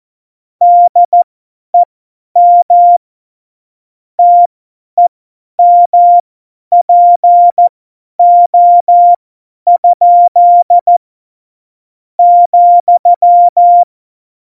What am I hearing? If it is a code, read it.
Morse code, heard as a keyed tone, about 14 wpm, spaced as for 10 wpm, 700 Hz. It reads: DEM TEMPO? ,
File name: Morsekurs / Witze / Witz 02 Juni 2020, 50 F70